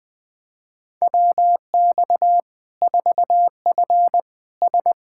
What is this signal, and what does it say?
Morse code, heard as a keyed tone, about 20 wpm, 700 Hz. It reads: WX 4F S
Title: Practice QSO - Exchange 2 - Info - 20wpm